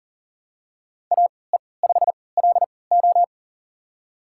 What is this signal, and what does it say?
Morse code, heard as a keyed tone, about 40 wpm, spaced as for 23 wpm, 690 Hz. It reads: AE5LO
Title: USA Call Signs - 3x Character Spacing - 40wpm